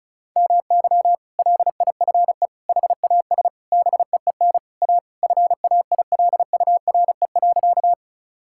Morse code, heard as a keyed tone, about 34 wpm, 700 Hz. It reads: MY LIFE HAS BEEN A FAILURE.
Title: Sentences from Top 1000 Words - Review Part 6 - 35wpm